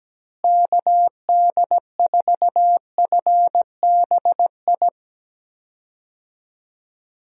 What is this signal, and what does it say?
Morse code, heard as a keyed tone, about 17 wpm, 700 Hz. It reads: KD4FBI
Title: USA Call Signs - 6 Characters - Repeated 3x - 17wpm